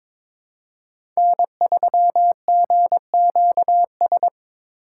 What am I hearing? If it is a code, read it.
Morse code, heard as a keyed tone, about 22 wpm, 700 Hz. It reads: N3GQS
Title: USA Call Signs - 8x Character Spacing - 22wpm